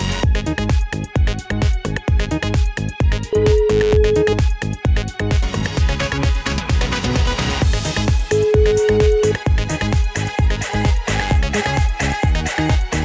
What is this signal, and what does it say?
Ringback tone (Europe, 425 Hz), with noise.